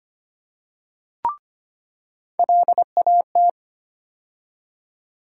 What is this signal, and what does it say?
Morse code, heard as a keyed tone, about 25 wpm, 700 Hz. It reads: LAT
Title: USA Call Sign Postfixes - 3 Letters 25wpm